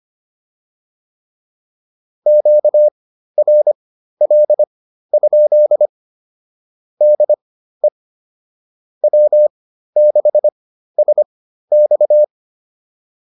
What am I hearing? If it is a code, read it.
Morse code, heard as a keyed tone, about 25 wpm, spaced as for 13 wpm, 600 Hz. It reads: QRL? DE W6SX